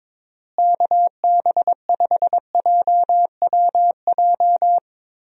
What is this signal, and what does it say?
Morse code, heard as a keyed tone, about 22 wpm, 700 Hz. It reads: KB5JWJ
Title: USA Call Signs - 6 Characters - Repeated 2x - 22wpm